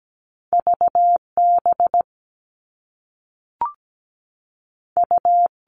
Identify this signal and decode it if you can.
Morse code, heard as a keyed tone, about 17 wpm, 700 Hz. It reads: VB U
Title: All Cut Numbers - 2-Digit 17wpm